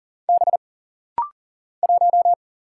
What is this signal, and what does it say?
Morse code, heard as a keyed tone, about 40 wpm, 700 Hz. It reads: B 1
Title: Single Letter-Number - Rapid-Fire 40wpm